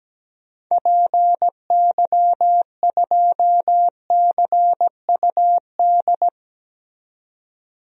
Morse code, heard as a keyed tone, about 17 wpm, 700 Hz. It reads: PY2CUD
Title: DXCC-100 Call Signs - 6 Characters - 17wpm